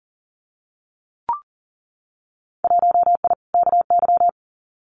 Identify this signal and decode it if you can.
Morse code, heard as a keyed tone, about 40 wpm, 700 Hz. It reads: 1I KY
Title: Winter Field Day Exchanges 40wpm